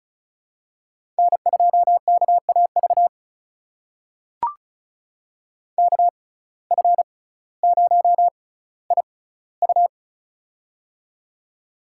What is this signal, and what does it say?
Morse code, heard as a keyed tone, about 36 wpm, spaced as for 18 wpm, 700 Hz. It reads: N2KAV KF0IU
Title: USA Call Signs - 5 Characters - 6x Character Spacing - 35wpm